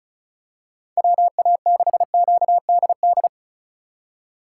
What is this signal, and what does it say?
Morse code, heard as a keyed tone, about 36 wpm, 700 Hz. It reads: WA6QDD
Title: USA Call Signs - 6 Characters - 4x Character Spacing - 35wpm